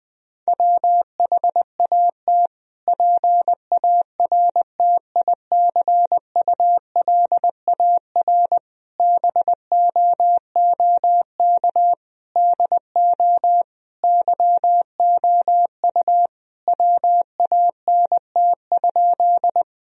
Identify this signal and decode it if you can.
Morse code, heard as a keyed tone, about 20 wpm, 700 Hz. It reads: WHAT PARTICULAR BOOK DO YOU WANT?
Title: Sentences from Top 1000 Words - Review Part 4 - 20wpm